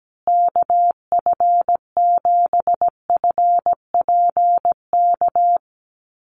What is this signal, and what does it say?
Morse code, heard as a keyed tone, about 17 wpm, 700 Hz. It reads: KF7FPK
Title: USA Call Signs Repeated 2x 17wpm